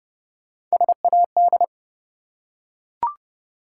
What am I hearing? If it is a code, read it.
Morse code, heard as a keyed tone, about 30 wpm, 700 Hz. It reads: SAD
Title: Three Letter Words - 30wpm